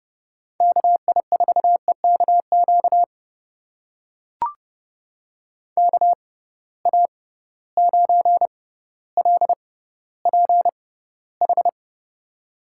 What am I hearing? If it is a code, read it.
Morse code, heard as a keyed tone, about 30 wpm, spaced as for 15 wpm, 700 Hz. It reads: KI4EKQ KA9LPH
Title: USA Call Signs - 6 Characters - 6x Character Spacing - 30wpm